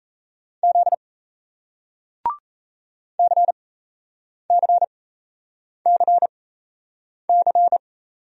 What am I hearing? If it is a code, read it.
Morse code, heard as a keyed tone, about 38 wpm, spaced as for 8 wpm, 700 Hz. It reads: Z CCCC